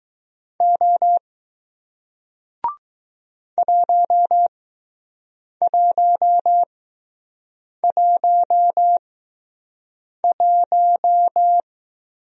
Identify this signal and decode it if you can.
Morse code, heard as a keyed tone, about 20 wpm, spaced as for 6 wpm, 700 Hz. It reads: O 1111